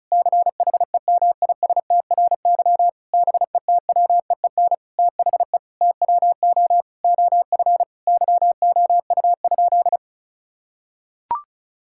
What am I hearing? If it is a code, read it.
Morse code, heard as a keyed tone, about 34 wpm, 700 Hz. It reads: CHEMISTRY BETWEEN THE TWO OF YOU?
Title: Sentences from Top 1000 Words - Review Part 6 - 35wpm